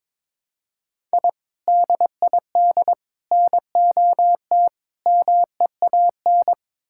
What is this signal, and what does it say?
Morse code, heard as a keyed tone, about 22 wpm, 700 Hz. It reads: I DID NOT MEAN
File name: practice / 22wpm / Build A Phrase - 5-Word Phrases - 22wpm